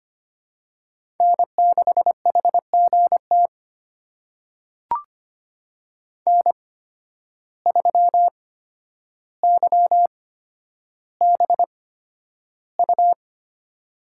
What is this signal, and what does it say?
Morse code, heard as a keyed tone, about 25 wpm, spaced as for 11 wpm, 700 Hz. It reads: N6HGT N3YBU
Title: USA Call Signs - 5 Characters - 8x Character Spacing - 25wpm